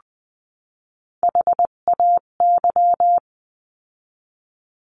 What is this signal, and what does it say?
Morse code, heard as a keyed tone, about 20 wpm, 700 Hz. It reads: HAY
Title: 258 Lesson 42c - Words 16-20wpm